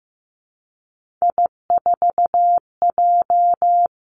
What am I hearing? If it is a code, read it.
Morse code, heard as a keyed tone, about 15 wpm, 700 Hz. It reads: I4J
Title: Three Challenging Letter-Number Combinations 15wpm